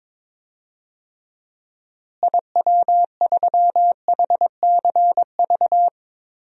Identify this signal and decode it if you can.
Morse code, heard as a keyed tone, about 22 wpm, 700 Hz. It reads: IW3HCV
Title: DXCC-100 Call Signs - 6 Characters - Repeated 2x - 22wpm